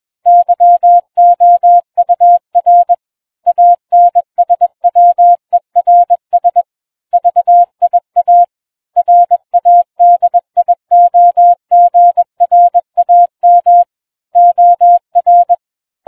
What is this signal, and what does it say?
Morse code, heard as a keyed tone, about 20 wpm, 700 Hz. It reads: YOUR ANSWERS VIA RADIOGRAM OR